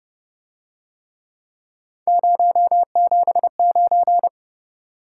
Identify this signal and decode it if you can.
Morse code, heard as a keyed tone, about 29 wpm, 700 Hz. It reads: 079